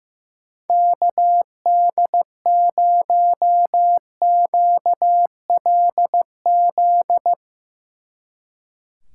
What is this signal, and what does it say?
Morse code, heard as a keyed tone, about 16 wpm, 700 Hz. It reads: KD0QLZ
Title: Speed Racing - USA 2x3 Call Signs 15wpm